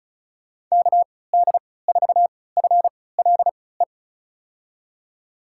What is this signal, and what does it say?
Morse code, heard as a keyed tone, about 36 wpm, spaced as for 20 wpm, 700 Hz. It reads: KD4FLE
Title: USA Call Signs - 6 Characters - 3x Character Spacing - 35wpm